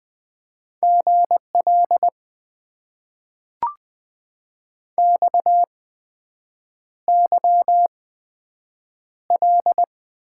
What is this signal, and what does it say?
Morse code, heard as a keyed tone, about 20 wpm, spaced as for 7 wpm, 700 Hz. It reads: GL XYL
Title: Top 90 QSO Elements - 8x Character Spacing - 20wpm